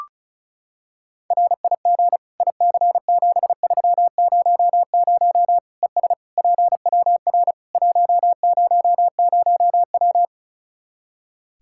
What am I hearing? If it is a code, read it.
Morse code, heard as a keyed tone, about 34 wpm, 700 Hz. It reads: RIG IC7300 ES PWR 100W
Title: Practice QSO - Exchange 2 - Info - 35wpm